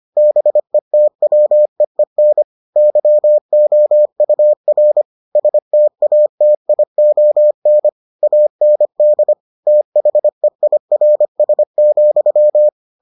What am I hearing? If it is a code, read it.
Morse code, heard as a keyed tone, about 25 wpm, 600 Hz. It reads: BETWEEN YOUR STATION AND THEIRS,